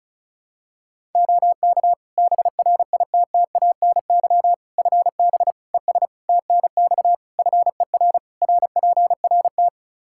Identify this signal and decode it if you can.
Morse code, heard as a keyed tone, about 36 wpm, 700 Hz. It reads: OK BRITTANY FB ES TNX FER RPRT
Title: Practice QSO - Exchange 2 - Info - 35wpm